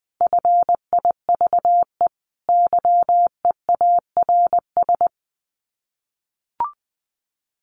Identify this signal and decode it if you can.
Morse code, heard as a keyed tone, about 20 wpm, 700 Hz. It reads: FIVE YEARS
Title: Sets of 5 Words Encore 20wpm